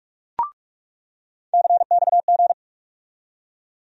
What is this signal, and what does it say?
Morse code, heard as a keyed tone, about 46 wpm, 700 Hz. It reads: CXG
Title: USA Call Sign Postfixes - 3 Letters 45wpm